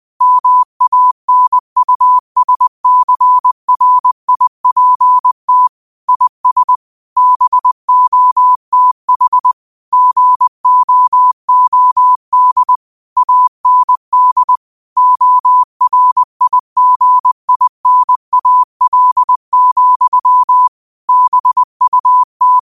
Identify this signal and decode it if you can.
Morse code, heard as a keyed tone, about 20 wpm, 1000 Hz. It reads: MANUSCRIPT IS BOTH GOOD AND ORIGINAL, BUT